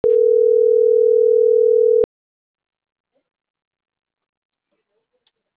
Ringback tone (North America), 440 + 480 Hz.